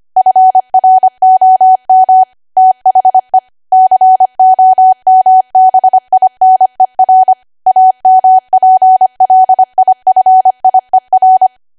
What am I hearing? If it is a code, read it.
Morse code, heard as a keyed tone, about 25 wpm, 750 Hz. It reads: FROM THE COMBINER AMPLIFIER
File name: Archive / 25 WPM / 25WPM